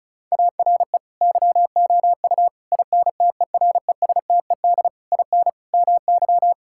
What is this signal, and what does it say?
Morse code, heard as a keyed tone, about 36 wpm, 700 Hz. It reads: ARE YOU INTERESTED IN MY
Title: Sentences from Top 1000 Words - Review Part 6 - 35wpm